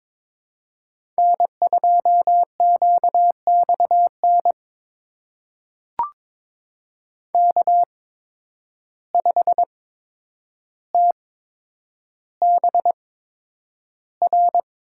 Morse code, heard as a keyed tone, about 22 wpm, spaced as for 9 wpm, 700 Hz. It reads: N2QXN K5TBR
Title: USA Call Signs - 5 Characters - 8x Character Spacing - 22wpm